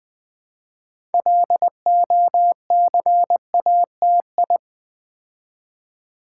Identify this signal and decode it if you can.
Morse code, heard as a keyed tone, about 20 wpm, 700 Hz. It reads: LOCATI
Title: Build A Word - 2 Letters At A Time - 8-Letter Words - 20wpm